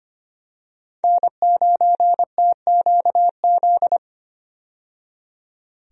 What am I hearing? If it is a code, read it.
Morse code, heard as a keyed tone, about 25 wpm, 700 Hz. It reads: N9TQZ
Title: USA Call Signs - 5 Characters - 25wpm